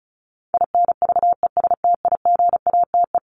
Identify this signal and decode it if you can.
Morse code, heard as a keyed tone, about 34 wpm, 700 Hz. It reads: INVESTIGATE
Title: Top 1751-2000 Words 35wpm